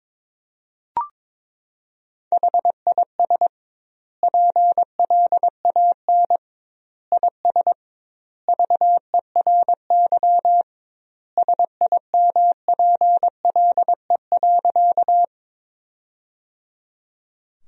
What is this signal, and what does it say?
Morse code, heard as a keyed tone, about 22 wpm, 700 Hz. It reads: HIS PLAN IS VERY SIMPLE.
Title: 2x Word-Spacing - Sentences from Top 500 Words - 22wpm